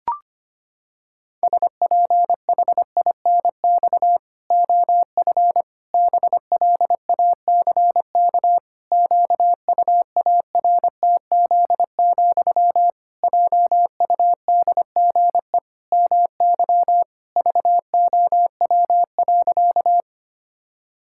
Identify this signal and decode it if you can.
Morse code, heard as a keyed tone, about 25 wpm, 700 Hz. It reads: SPHINX OF BLACK QUARTZ, JUDGE MY VOW.
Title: Pangram Challenge v2 - 25wpm